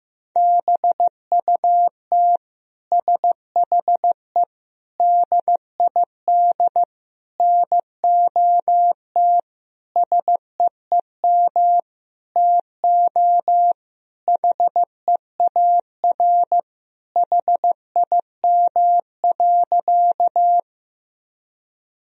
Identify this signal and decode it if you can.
Morse code, heard as a keyed tone, about 15 wpm, 700 Hz. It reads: BUT SHE DID NOT SEEM TO HEAR HIM.
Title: Sentences from Top 1000 Words - Review Part 1 - 15wpm